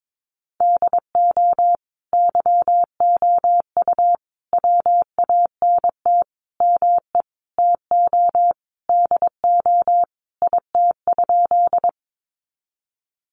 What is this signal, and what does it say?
Morse code, heard as a keyed tone, about 22 wpm, 700 Hz. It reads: DO YOU WANT ME TO DO IT?
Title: Sentences from Top 200 Words 22wpm